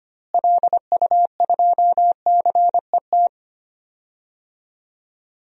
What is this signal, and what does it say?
Morse code, heard as a keyed tone, about 25 wpm, 700 Hz. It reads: LU2CET
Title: DXCC-100 Call Signs - 6 Characters - Repeated 2x - 25wpm